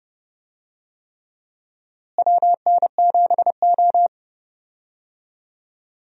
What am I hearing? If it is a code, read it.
Morse code, heard as a keyed tone, about 29 wpm, 700 Hz. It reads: WN7O
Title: USA Call Signs - 4 Characters - Repeated 3x - 30wpm